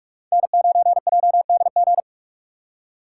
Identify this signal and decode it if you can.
Morse code, heard as a keyed tone, about 41 wpm, 700 Hz. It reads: N9JDG
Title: USA Call Signs - 5 Characters - 3x Character Spacing - 45wpm